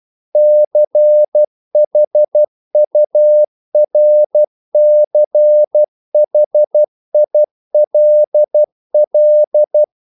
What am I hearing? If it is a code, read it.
Morse code, heard as a keyed tone, about 12 wpm, 600 Hz. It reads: CHURCHILL
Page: English quotations in CW (telegraphy - morse)